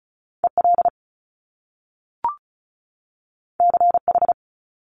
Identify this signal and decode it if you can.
Morse code, heard as a keyed tone, about 36 wpm, 700 Hz. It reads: EL CH
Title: Top Two Letter Combinations in Words 35wpm